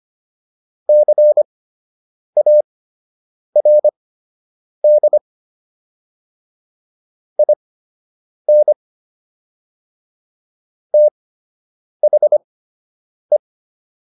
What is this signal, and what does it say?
Morse code, heard as a keyed tone, about 25 wpm, spaced as for 8 wpm, 600 Hz. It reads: CARD IN THE